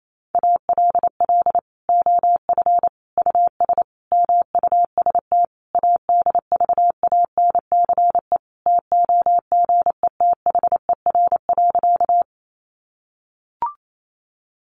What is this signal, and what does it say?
Morse code, heard as a keyed tone, about 28 wpm, 700 Hz. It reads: ALL OF US MUST ADVANCE TOGETHER.
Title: Sentences from Top 1400 Words 28wpm